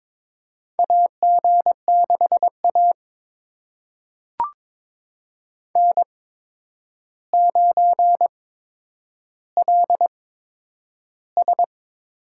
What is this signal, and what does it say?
Morse code, heard as a keyed tone, about 21 wpm, spaced as for 9 wpm, 700 Hz. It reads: AG6A N9LS
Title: USA Call Signs - 8x Character Spacing - 22wpm